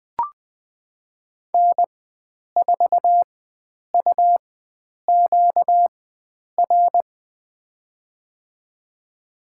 Morse code, heard as a keyed tone, about 20 wpm, spaced as for 9 wpm, 700 Hz. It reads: N4UQR
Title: USA Call Signs - 4x Character Spacing - 20wpm